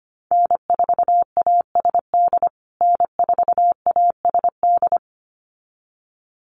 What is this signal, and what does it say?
Morse code, heard as a keyed tone, about 25 wpm, 700 Hz. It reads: N4ASD N4ASD
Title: USA Call Signs Repeated 2x 25wpm